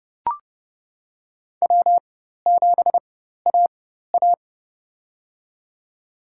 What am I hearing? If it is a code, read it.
Morse code, heard as a keyed tone, about 30 wpm, spaced as for 14 wpm, 700 Hz. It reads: W7AA
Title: USA Call Signs - 4 Characters - 4x Character Spacing - 30wpm